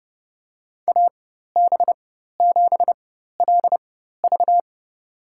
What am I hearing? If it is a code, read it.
Morse code, heard as a keyed tone, about 30 wpm, spaced as for 14 wpm, 700 Hz. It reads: AB7LV
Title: USA Call Signs - 4x Character Spacing - 30wpm